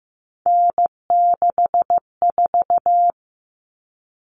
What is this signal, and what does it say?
Morse code, heard as a keyed tone, about 15 wpm, 700 Hz. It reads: N64
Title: Common Cut Numbers - 3-Digit 15wpm